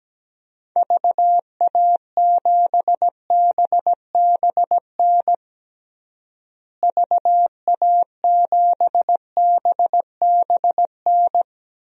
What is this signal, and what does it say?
Morse code, heard as a keyed tone, about 17 wpm, 700 Hz. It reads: VA7BBN VA7BBN
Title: DXCC-100 Call Signs - 6 Characters - Repeated 3x - 17wpm